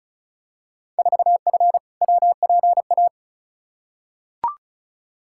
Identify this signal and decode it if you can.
Morse code, heard as a keyed tone, about 36 wpm, 700 Hz. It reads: 4F WPA
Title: Field Day Exchanges - 35wpm